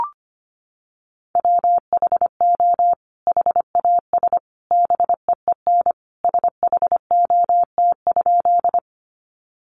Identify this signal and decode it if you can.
Morse code, heard as a keyed tone, about 25 wpm, 700 Hz. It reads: WHO HAS BEEN SHOT?
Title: Sentences from Top 1100 Words 25wpm